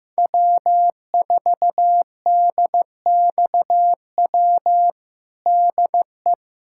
Morse code, heard as a keyed tone, about 15 wpm, 700 Hz. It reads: W4DXW DE
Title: USA Callsign DE Callsign - 5 Characters - 15wpm